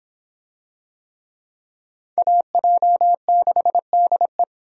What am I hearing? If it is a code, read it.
Morse code, heard as a keyed tone, about 25 wpm, 700 Hz. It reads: AJ6DE